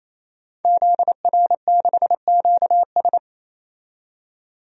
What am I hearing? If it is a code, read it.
Morse code, heard as a keyed tone, about 28 wpm, 700 Hz. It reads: ZR6QS
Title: DXCC-100 Call Signs - 5 Characters - 28wpm